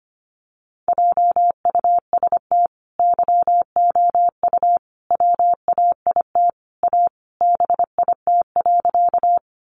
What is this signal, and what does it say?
Morse code, heard as a keyed tone, about 25 wpm, 700 Hz. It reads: JUST YOU WAIT A BIT.
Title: Sentences from Top 900 Words 25wpm